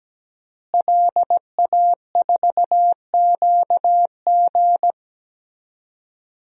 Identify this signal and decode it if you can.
Morse code, heard as a keyed tone, about 17 wpm, 700 Hz. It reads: LA4QG
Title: DXCC-100 Call Signs - 5 Characters - Repeated 2x - 17wpm